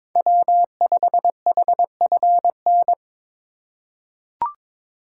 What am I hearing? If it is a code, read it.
Morse code, heard as a keyed tone, about 22 wpm, 700 Hz. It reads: W5HFN
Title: USA Call Signs - 8x Character Spacing - 22wpm